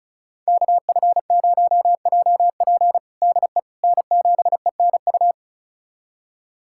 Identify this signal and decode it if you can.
Morse code, heard as a keyed tone, about 36 wpm, 700 Hz. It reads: KF0JP DE N7ENU
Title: USA Callsign DE Callsign - 5 Characters - 35wpm